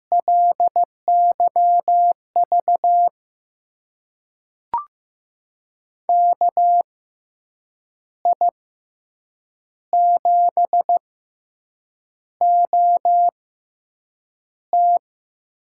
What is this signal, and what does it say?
Morse code, heard as a keyed tone, about 15 wpm, spaced as for 7 wpm, 700 Hz. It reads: LYV KI7OT
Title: USA Call Signs - 6 Characters - 6x Character Spacing - 15wpm